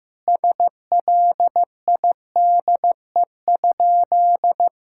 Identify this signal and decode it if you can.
Morse code, heard as a keyed tone, about 15 wpm, 700 Hz. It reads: SLIDE?
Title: Sentences from Top 1600 Words 15wpm